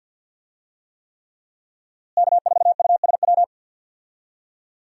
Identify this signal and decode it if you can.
Morse code, heard as a keyed tone, about 46 wpm, 700 Hz. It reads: K4USW
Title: USA Call Signs - 5 Characters - Repeated 3x - 50wpm